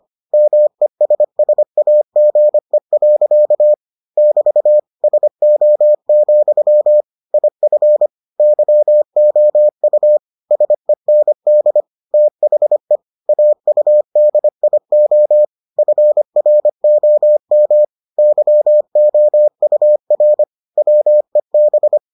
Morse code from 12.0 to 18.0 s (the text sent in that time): THE AUDIO FROM